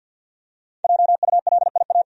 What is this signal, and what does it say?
Morse code, heard as a keyed tone, about 51 wpm, 700 Hz. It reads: JULIA
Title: USA Common Names - Max 5 Letters - 50wpm